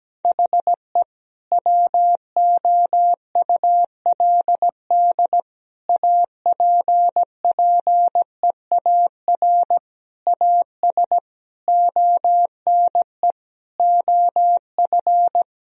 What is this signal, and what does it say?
Morse code, heard as a keyed tone, about 17 wpm, 700 Hz. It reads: HE WOULD APPEAR AS ONE OF
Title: Sentences from Top 1000 Words - Review Part 2 - 17wpm